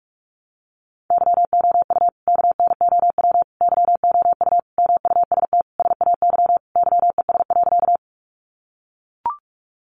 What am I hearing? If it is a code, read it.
Morse code, heard as a keyed tone, about 44 wpm, 700 Hz. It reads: YOU KNOW YOU MUST SAY YES.
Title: Sentences from Top 500 Words 45wpm